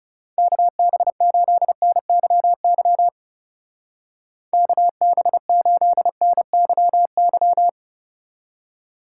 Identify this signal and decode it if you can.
Morse code, heard as a keyed tone, about 37 wpm, 700 Hz. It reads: KB8NYY KB8NYY
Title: Speed Racing - USA 2x3 Call Signs 30wpm